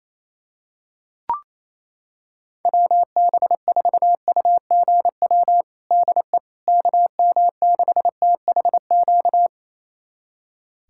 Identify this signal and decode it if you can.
Morse code, heard as a keyed tone, about 28 wpm, 700 Hz. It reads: WB4UGW DE KM6THQ
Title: USA Callsign DE Callsign - 6 Characters - 28wpm